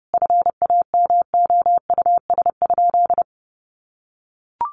Morse code, heard as a keyed tone, about 30 wpm, 700 Hz. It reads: FAMOUS?
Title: Sentences from Top 1500 Words 30wpm